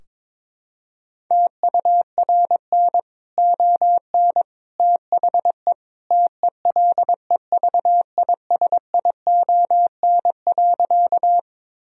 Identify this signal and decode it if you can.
Morse code, heard as a keyed tone, about 22 wpm, 700 Hz. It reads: TURN ON THE TELEVISION.
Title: Sentences from Top 1000 Words - Review Part 4 - 22wpm